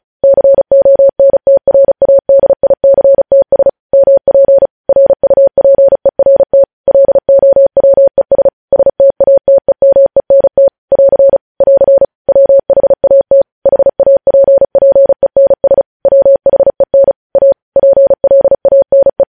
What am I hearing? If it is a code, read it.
Morse code, heard as a keyed tone, about 36 wpm, 550 Hz. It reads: CONTRADICTS MP RUPERT LOWES STATEMENT <AR> <AR> WHAT HAPPENS WHEN A PLANE
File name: podcast / News headlines-1x-35x35